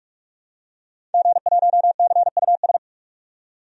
Morse code, heard as a keyed tone, about 46 wpm, 700 Hz. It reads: G1CUS
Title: DXCC-100 Call Signs - 5 Characters - 45wpm